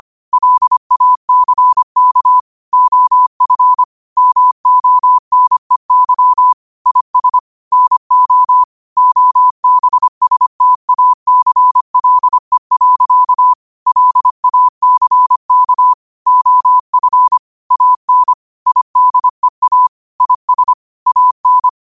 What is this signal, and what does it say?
Morse code, heard as a keyed tone, about 25 wpm, 1000 Hz. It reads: LACK OF MONEY IS NO OBSTACLE. LACK OF AN IDEA IS AN